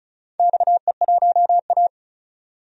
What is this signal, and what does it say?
Morse code, heard as a keyed tone, about 34 wpm, 690 Hz. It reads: XE1A
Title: DXCC-100 Call Signs - 4 Characters - 35wpm